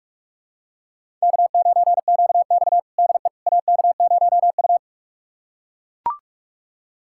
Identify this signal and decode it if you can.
Morse code, heard as a keyed tone, about 42 wpm, 700 Hz. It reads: K9QX DE AK0U
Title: USA Callsign DE Callsign - 4 Characters - 45wpm